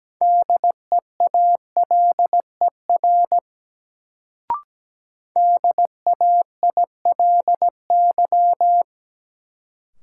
Morse code, heard as a keyed tone, about 17 wpm, 700 Hz. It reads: DEALER DAILY
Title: Sentences from Top 1100 Words - New Word Drill - 17wpm